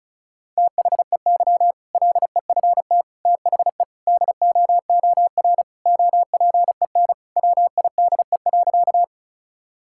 Morse code, heard as a keyed tone, about 36 wpm, 700 Hz. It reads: THEY LEFT THE DOOR OPEN WIDE.
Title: Sentences from Top 1000 Words - Review Part 6 - 35wpm